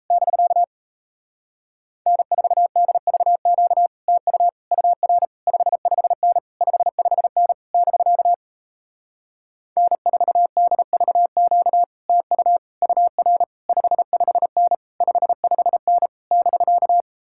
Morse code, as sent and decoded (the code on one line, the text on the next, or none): -...-.- / -. ....- -.. ...- --.- / - ..- / ..- .-. / ..... ..... -. / ..... ..... -. / -...-.- / -. ....- -.. ...- --.- / - ..- / ..- .-. / ..... ..... -. / ..... ..... -. / -...-.-
<BK> N4DVQ TU UR 55N 55N <BK> N4DVQ TU UR 55N 55N <BK>